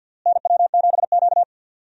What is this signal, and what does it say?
Morse code, heard as a keyed tone, about 52 wpm, 700 Hz. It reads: NW7Q
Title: USA Call Signs - 4 Characters - Repeated 3x - 50wpm